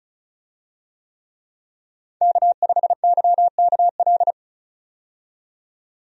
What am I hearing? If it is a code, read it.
Morse code, heard as a keyed tone, about 36 wpm, 700 Hz. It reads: K5YKL
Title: USA Call Signs - 5 Characters - 35wpm